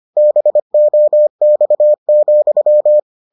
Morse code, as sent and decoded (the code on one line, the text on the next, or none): -... --- -..- --..--
BOX,